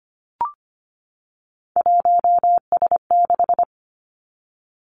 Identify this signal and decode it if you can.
Morse code, heard as a keyed tone, about 25 wpm, 700 Hz. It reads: 1S6
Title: Three Challenging Letter-Number Combinations 25wpm